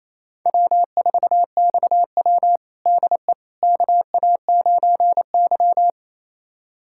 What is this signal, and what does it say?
Morse code, heard as a keyed tone, about 28 wpm, 700 Hz. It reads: W4XW DE KA9Y
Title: USA Callsign DE Callsign - 4 Characters - 28wpm